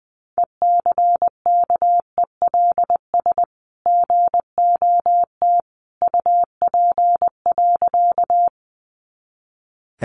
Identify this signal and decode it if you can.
Morse code, heard as a keyed tone, about 20 wpm, 700 Hz. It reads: ECKELS GOT UP.
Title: A Sound Of Thunder 20wpm